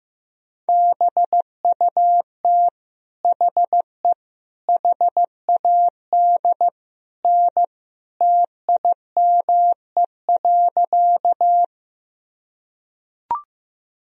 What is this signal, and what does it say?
Morse code, heard as a keyed tone, about 15 wpm, 700 Hz. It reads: BUT HE HAD N TIME.
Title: Sentences from Top 100 Words and Abbreviations 15wpm